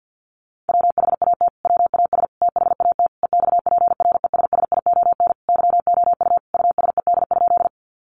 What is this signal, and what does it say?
Morse code, heard as a keyed tone, about 50 wpm, 700 Hz. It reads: WHAT WAS THAT EXPRESSION YOU USED?
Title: Sentences from Top 1200 Words 50wpm